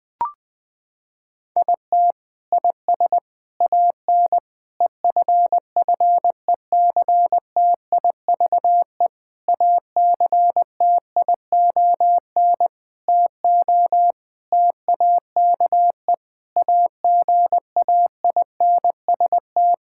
Morse code, heard as a keyed tone, about 20 wpm, 700 Hz. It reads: IT IS AN EFFECTIVE ACTION TO TAKE AGAINST
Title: Sentences from Top 1000 Words - Review Part 6 - 20wpm